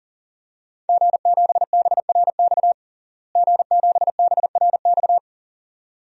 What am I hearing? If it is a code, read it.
Morse code, heard as a keyed tone, about 37 wpm, 700 Hz. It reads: G7BRX G7BRX
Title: DXCC-100 Call Signs - 5 Characters - Repeated 2x - 40wpm